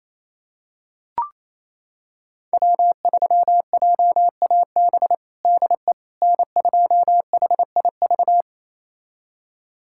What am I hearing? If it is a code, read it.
Morse code, heard as a keyed tone, about 28 wpm, 700 Hz. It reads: W3JAB DE N2HIV
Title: USA Callsign DE Callsign - 5 Characters - 28wpm